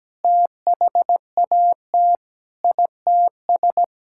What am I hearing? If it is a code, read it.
Morse code, heard as a keyed tone, about 17 wpm, 700 Hz. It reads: THAT ITS
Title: Sentences from Top 1000 Words - Review Part 6 - 17wpm